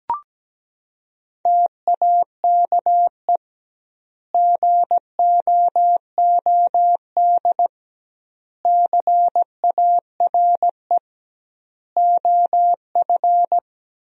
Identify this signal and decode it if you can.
Morse code, heard as a keyed tone, about 17 wpm, 700 Hz. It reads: TAKE GOOD CARE OF
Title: 2x Word-Spacing - Sentences from Top 400 Words - 17wpm